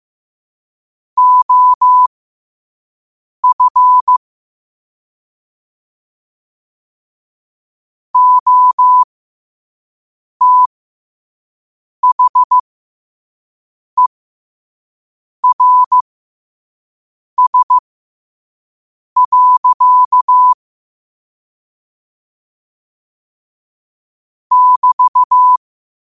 Morse code, heard as a keyed tone, about 15 wpm, spaced as for 5 wpm, 1000 Hz. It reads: OF OTHERS. <BT>
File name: qotdfiles / 5wpm